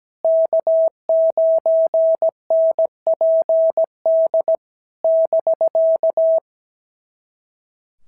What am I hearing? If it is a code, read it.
Morse code, heard as a keyed tone, about 17 wpm, 650 Hz. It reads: K9NPD <BK>
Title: SOTA QSO Practice 17wpm